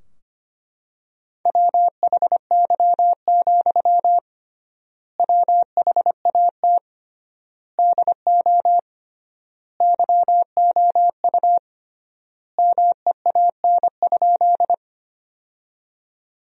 Morse code, heard as a keyed tone, about 25 wpm, 700 Hz. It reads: WHY, WHAT DO YOU MEAN?
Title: 3x Word-Spacing - Sentences from Top 200 Words - 25wpm